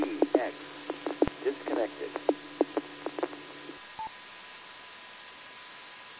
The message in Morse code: none

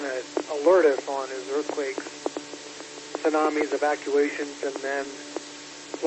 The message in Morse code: none

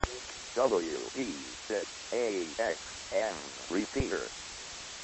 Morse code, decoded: none

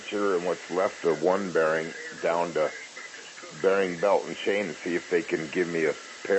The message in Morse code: none

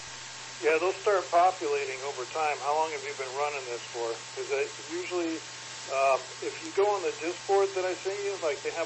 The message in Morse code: none

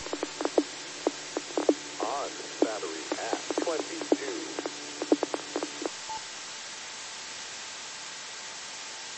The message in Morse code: none

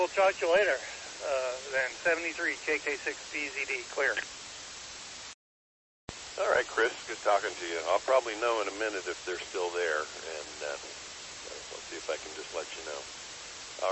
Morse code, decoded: none

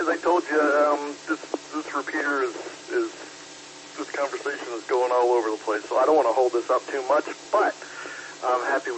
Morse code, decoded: none

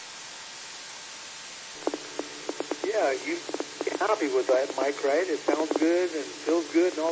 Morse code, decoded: none

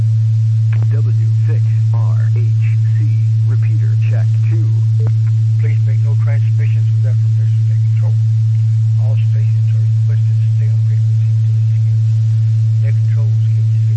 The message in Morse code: none

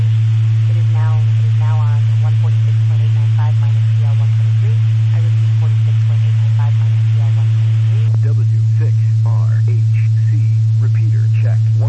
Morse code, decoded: none

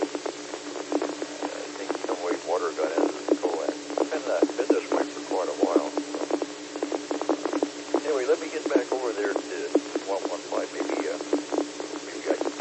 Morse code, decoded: none

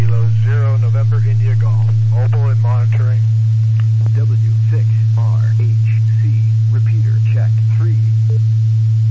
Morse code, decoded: none

